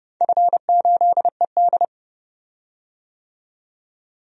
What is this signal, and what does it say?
Morse code, heard as a keyed tone, about 30 wpm, 700 Hz. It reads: F8ED
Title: DXCC-100 Call Signs - 4 Characters - 30wpm